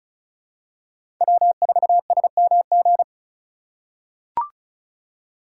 Morse code, heard as a keyed tone, about 36 wpm, 700 Hz. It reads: W4SMG
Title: USA Call Signs - 5 Characters - 3x Character Spacing - 35wpm